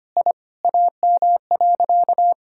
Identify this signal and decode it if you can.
Morse code, heard as a keyed tone, about 25 wpm, 700 Hz. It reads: I AM.
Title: Sentences from Top 1000 Words - Review Part 5 - 25wpm